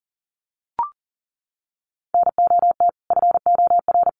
Morse code, heard as a keyed tone, about 40 wpm, 700 Hz. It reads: NOT FOR
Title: Sentences from Top 1300 Words 40wpm